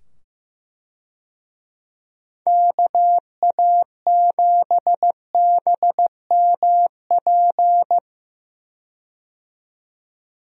Morse code, heard as a keyed tone, about 15 wpm, 700 Hz. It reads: KA7BMP
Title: USA Call Signs - 6 Characters - 15wpm